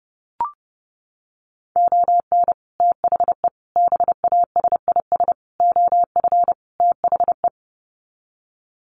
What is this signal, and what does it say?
Morse code, heard as a keyed tone, about 30 wpm, 700 Hz. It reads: ON THE BASIS OF THE
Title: Sets of 5 Words 30wpm